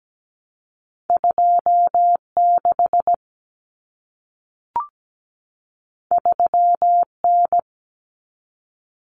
Morse code, heard as a keyed tone, about 17 wpm, 700 Hz. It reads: 26 3N
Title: Common Cut Numbers - 2-Digit 17wpm